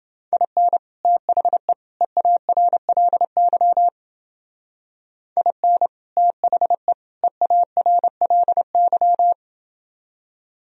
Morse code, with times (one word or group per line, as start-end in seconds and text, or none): 0.3-0.8 IN
1.1-1.7 THE
2.0-3.9 EARLY
5.4-5.9 IN
6.2-6.9 THE
7.2-9.3 EARLY